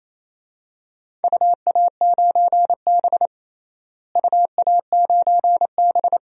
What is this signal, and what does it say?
Morse code, heard as a keyed tone, about 28 wpm, 700 Hz. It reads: UA9B UA9B
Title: DXCC-100 Call Signs - 4 Characters - Repeated 2x - 28wpm